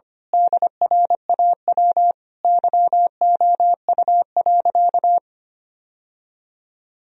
Morse code, heard as a keyed tone, about 25 wpm, 700 Hz. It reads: DRAW YOU.
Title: Sentences from Top 1000 Words - Review Part 1 - 25wpm